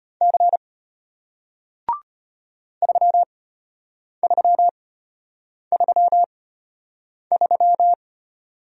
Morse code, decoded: C 3333

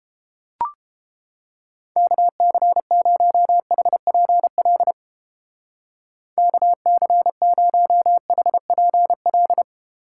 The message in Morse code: KC0HPL KC0HPL

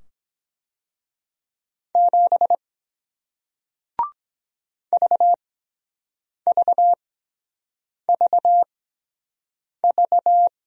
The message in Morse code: 7 VVVV

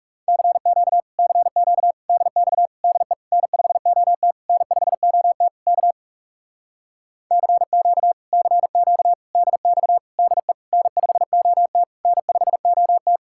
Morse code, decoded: CQ CQ DX DE N5OT N5OT K CQ CQ DX DE N5OT N5OT

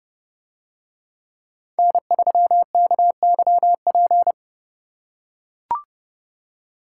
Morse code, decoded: N3KYP

